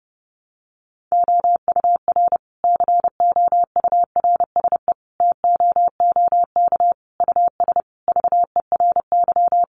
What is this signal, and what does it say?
Morse code, heard as a keyed tone, about 30 wpm, 700 Hz. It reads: OUR COURSE TOOK US VERY
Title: Sentences from Top 1300 Words 30wpm